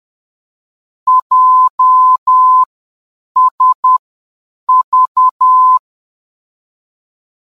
Morse code, heard as a keyed tone, about 10 wpm, spaced as for 7 wpm, 1020 Hz. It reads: JSV
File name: JSV-VOR DME